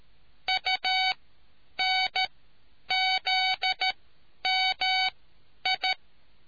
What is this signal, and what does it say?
Morse code, heard as a keyed tone, about 13 wpm, spaced as for 9 wpm, 750 Hz. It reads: UNZMI